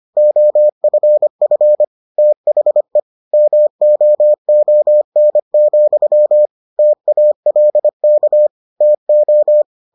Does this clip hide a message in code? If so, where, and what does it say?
Yes: Morse code from 0.2 to 9.6 s, heard as a keyed tone, about 25 wpm, 600 Hz. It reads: OFF THE MOON, TALK TO